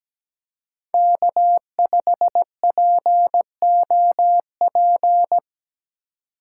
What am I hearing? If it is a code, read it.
Morse code, heard as a keyed tone, about 17 wpm, 700 Hz. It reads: K5POP
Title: USA Call Signs - 5 Characters - 2x Character Spacing - 17wpm